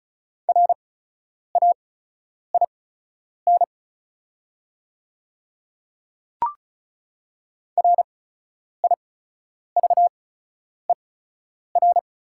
Morse code, heard as a keyed tone, about 34 wpm, spaced as for 10 wpm, 690 Hz. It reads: RAIN RIVER
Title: 8x Character Spacing - Top 500 Words - 35wpm